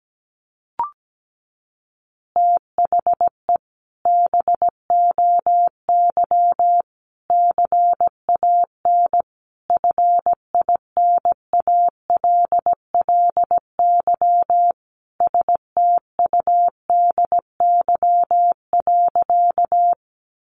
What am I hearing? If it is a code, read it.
Morse code, heard as a keyed tone, about 17 wpm, 700 Hz. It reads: THE BOY CAN FINALLY STUDY.
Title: Additional Easy Phrases 17wpm